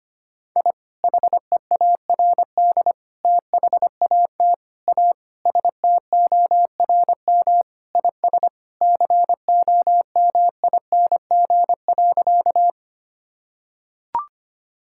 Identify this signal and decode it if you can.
Morse code, heard as a keyed tone, about 25 wpm, 700 Hz. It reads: I HEARD THAT A STORM IS COMING.
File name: practice / 25wpm / Sentences with 2-Word Phrases - 25wpm